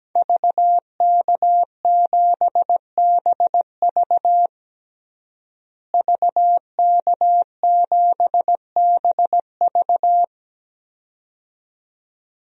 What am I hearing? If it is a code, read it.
Morse code, heard as a keyed tone, about 17 wpm, 700 Hz. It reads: VK7BV VK7BV
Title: DXCC-100 Call Signs - 5 Characters - Repeated 2x - 17wpm